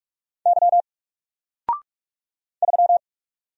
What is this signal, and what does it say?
Morse code, heard as a keyed tone, about 46 wpm, 700 Hz. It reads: Y 3
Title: Single Letter-Number 45wpm